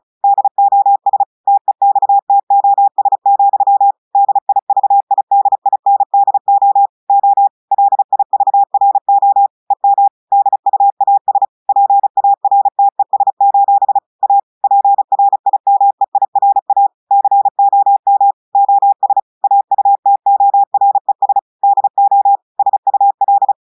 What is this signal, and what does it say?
Morse code, heard as a keyed tone, about 36 wpm, 800 Hz. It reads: DOS TEXTOS, DIVIDINDO O LIVRO EM DUAS PARTES: A PRIMEIRA COM OS AUTORES DO SUL
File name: audios / portugais / portugais texte en clair 1 vitesse 35 mots minutes